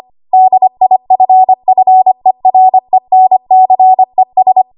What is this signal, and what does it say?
Morse code, heard as a keyed tone, about 25 wpm, 750 Hz. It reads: DIFFERENCES
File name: Archive / 25 WPM / 25WPM